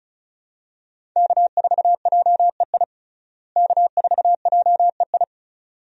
Morse code, heard as a keyed tone, about 36 wpm, 700 Hz. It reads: K4JEI K4JEI
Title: USA Call Signs - 5 Characters - Repeated 2x - 35wpm